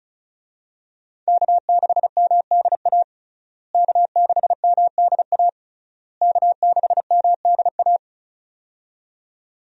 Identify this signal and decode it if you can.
Morse code, heard as a keyed tone, about 34 wpm, 700 Hz. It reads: K6MDA K6MDA K6MDA
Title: USA Call Signs - 5 Characters - Repeated 3x - 35wpm